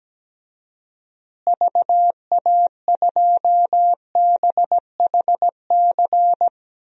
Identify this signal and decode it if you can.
Morse code, heard as a keyed tone, about 17 wpm, 700 Hz. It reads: VA2BHC
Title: DXCC-100 Call Signs - 6 Characters - Repeated 2x - 17wpm